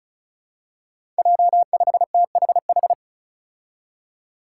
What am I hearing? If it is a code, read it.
Morse code, heard as a keyed tone, about 34 wpm, 700 Hz. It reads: J5THH
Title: Five Challenging Letter-Number Combinations 35wpm